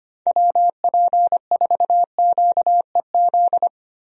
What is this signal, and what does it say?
Morse code, heard as a keyed tone, about 25 wpm, 700 Hz. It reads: WP4QEZ
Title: USA Call Signs - 6 Characters - Repeated 2x - 25wpm